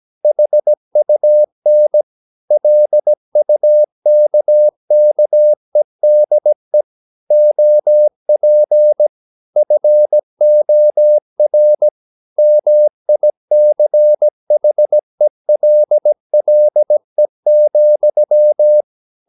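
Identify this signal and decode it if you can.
Morse code, heard as a keyed tone, about 17 wpm, 600 Hz. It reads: HUN LUKKEDE OP FOR MICHELLE,
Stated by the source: Slow (QRS) Morse practice at 17 wpm, from CW med Gnister.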